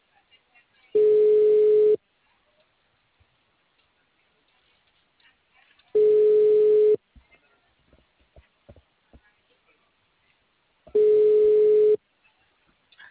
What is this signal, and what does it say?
Ringback tone (Europe, 425 Hz), with speech.